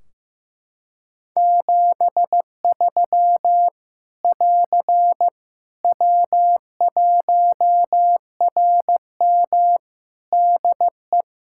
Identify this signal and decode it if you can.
Morse code, heard as a keyed tone, about 15 wpm, 700 Hz. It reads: 73 <AR> W1RM DE
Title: Practice QSO - Exchange 4 - Ending - 15wpm